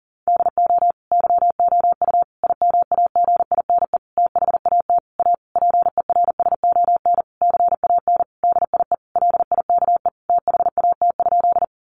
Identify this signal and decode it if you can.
Morse code, heard as a keyed tone, about 37 wpm, 700 Hz. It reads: DO YOU IMAGINE THAT A PERSON CAN DIE LIKE THAT?
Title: Sentences from Top 1400 Words 40wpm